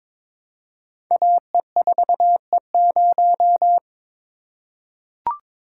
Morse code, heard as a keyed tone, about 22 wpm, 700 Hz. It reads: AE4E0
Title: Code Groups - 5 Letter:Numbers - 22wpm